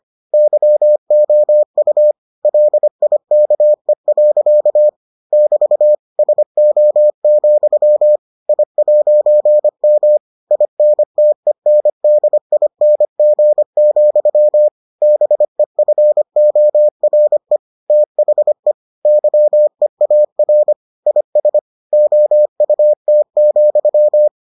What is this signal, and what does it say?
Morse code, heard as a keyed tone, about 25 wpm, 600 Hz. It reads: YOU LIKE. <BT> SO, I'M INTENDING, BEFORE THE YEAR IS OUT,